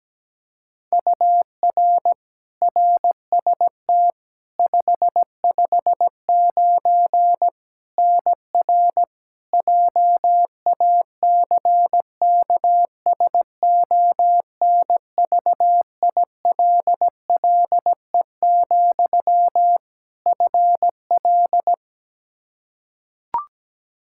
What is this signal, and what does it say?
Morse code, heard as a keyed tone, about 17 wpm, 700 Hz. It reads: UR RST 559 NR JACKSONVILLE, FL
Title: Practice QSO - Exchange 1 - Beginning - 17wpm